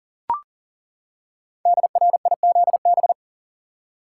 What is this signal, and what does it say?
Morse code, heard as a keyed tone, about 40 wpm, 700 Hz. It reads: DRIZB
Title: Code Groups - 5 Letters - 40wpm